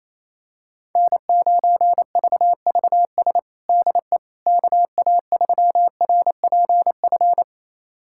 Morse code, heard as a keyed tone, about 28 wpm, 700 Hz. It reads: N9VVS DE KA3RPF
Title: USA Callsign DE Callsign 28wpm